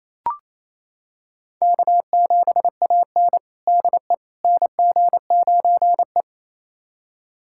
Morse code, heard as a keyed tone, about 28 wpm, 700 Hz. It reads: K7AN DE NG9E
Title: USA Callsign DE Callsign - 4 Characters - 28wpm